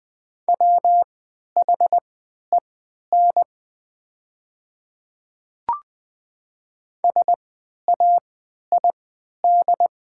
Morse code, heard as a keyed tone, about 20 wpm, spaced as for 11 wpm, 700 Hz. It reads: WHEN SAID
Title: 3x Character Spacing - Top 200 Words - 20wpm